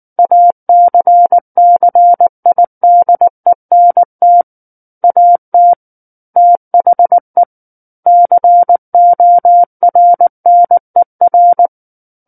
Morse code, heard as a keyed tone, about 19 wpm, 700 Hz. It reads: ACCIDENT AT THE CORNER
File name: Book One / 10-Chapter 10 - 19 WPM